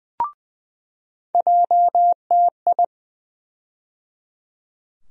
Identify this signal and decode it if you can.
Morse code, heard as a keyed tone, about 20 wpm, 700 Hz. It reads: JTI
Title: USA Call Sign Postfixes - 3 Letters 20wpm